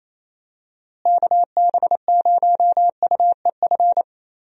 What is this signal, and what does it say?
Morse code, heard as a keyed tone, about 27 wpm, 700 Hz. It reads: KB0UEF
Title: USA Call Signs - 6 Characters - Repeated 2x - 28wpm